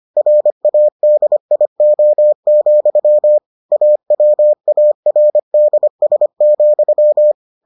Morse code, heard as a keyed tone, about 25 wpm, 600 Hz. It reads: RADIO, AWARDS,